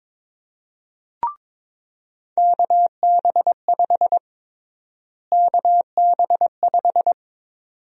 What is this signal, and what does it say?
Morse code, heard as a keyed tone, about 22 wpm, 700 Hz. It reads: KB5 KB5
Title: USA Call Sign Prefixes - Repeated 2x - 22wpm